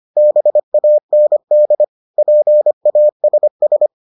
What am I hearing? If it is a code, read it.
Morse code, heard as a keyed tone, about 25 wpm, 600 Hz. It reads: BAND PASS